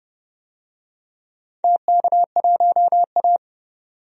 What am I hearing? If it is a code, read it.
Morse code, heard as a keyed tone, about 30 wpm, 700 Hz. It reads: TK1A
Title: DXCC-100 Call Signs - 4 Characters - 30wpm